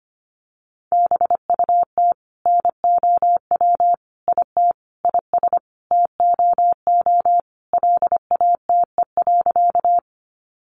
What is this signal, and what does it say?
Morse code, heard as a keyed tone, about 25 wpm, 700 Hz. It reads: BUT NOW IT IS TOO LATE.
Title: Sentences from Top 300 Words 25wpm